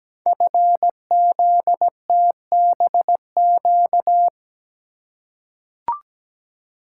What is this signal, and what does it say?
Morse code, heard as a keyed tone, about 17 wpm, 700 Hz. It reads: FZTBQ
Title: Code Groups - 5 Letters - 17wpm